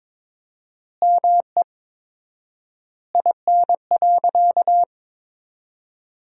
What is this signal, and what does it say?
Morse code, heard as a keyed tone, about 23 wpm, 700 Hz. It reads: ME IN.
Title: 4x Word-Spacing - Sentences from Top 300 Words - 22wpm